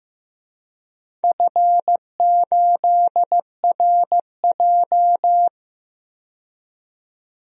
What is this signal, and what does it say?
Morse code, heard as a keyed tone, about 15 wpm, 700 Hz. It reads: F8RJ
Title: DXCC-100 Call Signs - 4 Characters - 15wpm